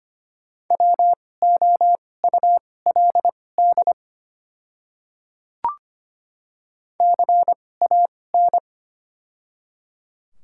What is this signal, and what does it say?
Morse code, heard as a keyed tone, about 25 wpm, spaced as for 18 wpm, 700 Hz. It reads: WOULD CAN
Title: 2x Character Spacing - Top 100 Words - 25wpm